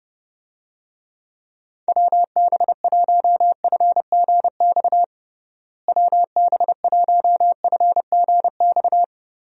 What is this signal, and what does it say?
Morse code, heard as a keyed tone, about 32 wpm, 700 Hz. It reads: WB1FGX WB1FGX
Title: USA Call Signs - 6 Characters - Repeated 3x - 30wpm